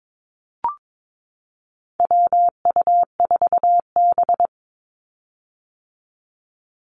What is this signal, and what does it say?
Morse code, heard as a keyed tone, about 22 wpm, 700 Hz. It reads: WU4B
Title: Call Signs - 22wpm